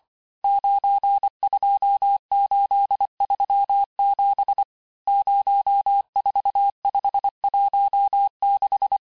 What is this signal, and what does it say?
Morse code, heard as a keyed tone, about 25 wpm, 800 Hz. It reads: 92837 04516
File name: texte-chiffre-vitesse-23mpm